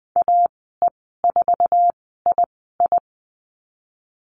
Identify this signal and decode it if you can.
Morse code, heard as a keyed tone, about 20 wpm, spaced as for 15 wpm, 700 Hz. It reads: AE4II
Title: 069 Lesson 12g - Letter d - Call Signs 20wpm